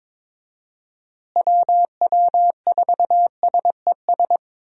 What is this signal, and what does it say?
Morse code, heard as a keyed tone, about 22 wpm, 700 Hz. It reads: WW4SES
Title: USA Call Signs - 6 Characters - Repeated 2x - 22wpm